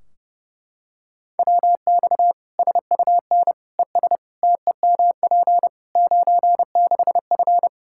Morse code, heard as a keyed tone, about 30 wpm, 700 Hz. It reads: WX SUN ES TEMP 96F
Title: Practice QSO - Exchange 2 - Info - 30wpm